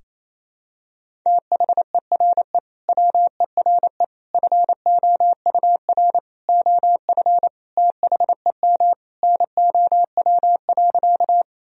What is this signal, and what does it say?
Morse code, heard as a keyed tone, about 28 wpm, 700 Hz. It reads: THERE WERE FOUR OF THEM NOW.
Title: Sentences from Top 1000 Words - Review Part 1 - 28wpm